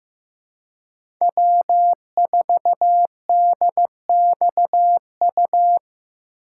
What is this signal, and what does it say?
Morse code, heard as a keyed tone, about 16 wpm, 700 Hz. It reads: W4DXU
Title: USA Call Signs - 5 Characters - 2x Character Spacing - 15wpm